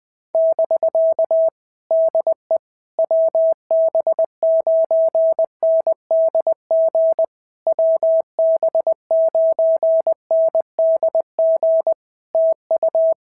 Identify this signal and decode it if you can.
Morse code, heard as a keyed tone, about 20 wpm, 650 Hz. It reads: <BK> DE WB9NDG WB9NDG TU
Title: POTA QSO Practice - 20wpm